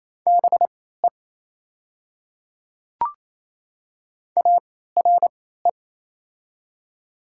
Morse code, heard as a keyed tone, about 28 wpm, spaced as for 16 wpm, 690 Hz. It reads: BE ARE
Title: 3x Character Spacing - Top 100 Words - 28wpm